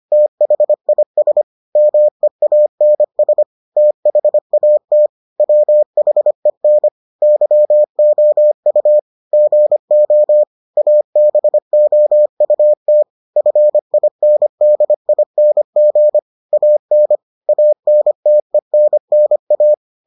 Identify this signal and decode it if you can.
Morse code, heard as a keyed tone, about 25 wpm, 600 Hz. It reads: THIS MEANS THAT WHEN YOU GO ABOUT FINDING AN ANTENNA